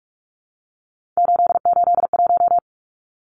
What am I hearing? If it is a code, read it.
Morse code, heard as a keyed tone, about 44 wpm, 700 Hz. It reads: 881
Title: Numbers - 3-Digit 45wpm